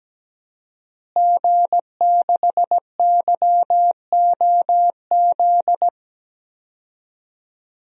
Morse code, heard as a keyed tone, about 17 wpm, 700 Hz. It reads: G6YOZ
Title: DXCC-100 Call Signs - 5 Characters - Repeated 2x - 17wpm